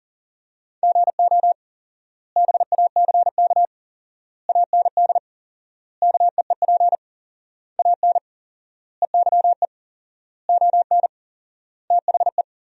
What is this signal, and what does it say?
Morse code, heard as a keyed tone, about 37 wpm, 700 Hz. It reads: GO BACK AND KEEP AN EYE ON THE
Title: 4x Word-Spacing - Sentences from Top 300 Words - 40wpm